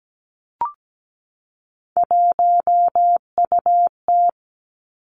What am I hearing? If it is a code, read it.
Morse code, heard as a keyed tone, about 17 wpm, 700 Hz. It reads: 1UT
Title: Three Challenging Letter-Number Combinations 17wpm